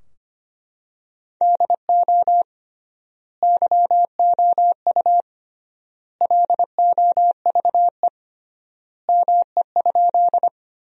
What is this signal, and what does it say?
Morse code, heard as a keyed tone, about 25 wpm, 700 Hz. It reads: DO YOU LOVE ME?
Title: 3x Word-Spacing - Sentences from Top 400 Words - 25wpm